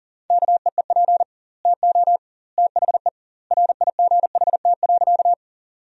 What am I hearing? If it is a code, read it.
Morse code, heard as a keyed tone, about 40 wpm, 700 Hz. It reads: KEEP TO THE RIGHT.
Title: 2x Word-Spacing - Sentences from Top 300 Words - 40wpm